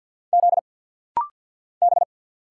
Morse code, heard as a keyed tone, about 50 wpm, 700 Hz. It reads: Z B
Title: Single Letters - Rapid-Fire 50wpm